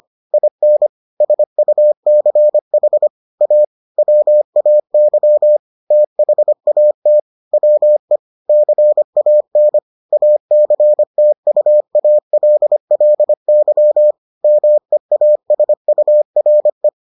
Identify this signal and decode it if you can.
Morse code, heard as a keyed tone, about 25 wpm, 600 Hz. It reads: IN SUCH A WAY THAT WE CAN ACTUALLY MEASURE